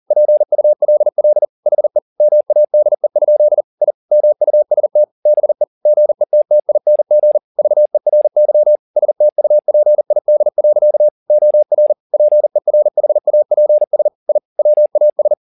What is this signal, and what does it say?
Morse code, heard as a keyed tone, about 40 wpm, 600 Hz. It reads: PULL HE MADE? I MUST BE GETTING VERY STUPID. OR PERHAPS I WAS